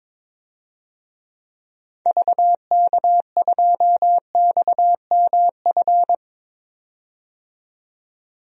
Morse code, heard as a keyed tone, about 22 wpm, 700 Hz. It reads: VK2XMF
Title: DXCC-100 Call Signs - 6 Characters - 22wpm